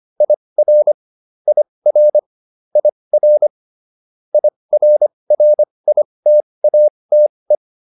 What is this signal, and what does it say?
Morse code, heard as a keyed tone, about 25 wpm, spaced as for 20 wpm, 600 Hz. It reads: IR IR IR IRRITATE